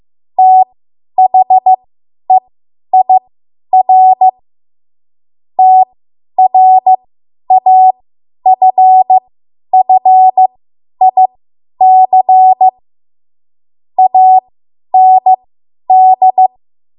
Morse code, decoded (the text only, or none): THEIR TRAFFIC AND